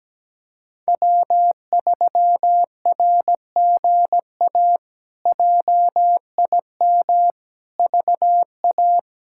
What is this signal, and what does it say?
Morse code, heard as a keyed tone, about 17 wpm, 700 Hz. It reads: W3RGA JIM VA
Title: K1USN SST QSO Practice - 17wpm